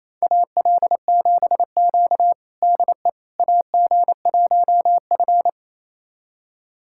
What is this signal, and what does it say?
Morse code, heard as a keyed tone, about 28 wpm, 700 Hz. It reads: AL7Q DE AG1F
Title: USA Callsign DE Callsign - 4 Characters - 28wpm